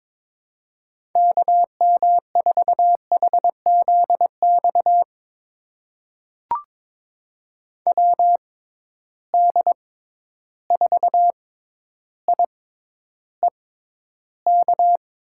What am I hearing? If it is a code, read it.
Morse code, heard as a keyed tone, about 22 wpm, spaced as for 11 wpm, 700 Hz. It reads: KM4HZX WD4IEK
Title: USA Call Signs - 6 Characters - 6x Character Spacing - 22wpm